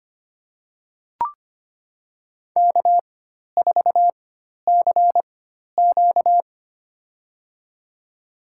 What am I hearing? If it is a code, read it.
Morse code, heard as a keyed tone, about 25 wpm, spaced as for 12 wpm, 700 Hz. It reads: K4CQ
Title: USA Call Signs - 4 Characters - 4x Character Spacing - 25wpm